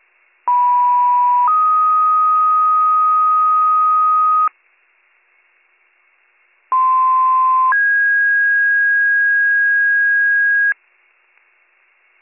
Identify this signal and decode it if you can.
Two-tone page, heard as a one-second tone, then a three-second tone, about 980 then 1250 Hz, about 1040 then 1650 Hz. It is heard from 0.5 to 10.7 s.